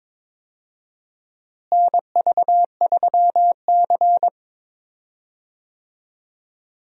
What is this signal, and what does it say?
Morse code, heard as a keyed tone, about 22 wpm, 700 Hz. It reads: NV3C
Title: USA Call Signs - 4 Characters - Repeated 2x - 22wpm